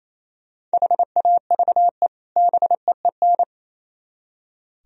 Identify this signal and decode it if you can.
Morse code, heard as a keyed tone, about 34 wpm, 700 Hz. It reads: HAVE BEEN
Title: Speed-Racing - Sets of 2 Words 28wpm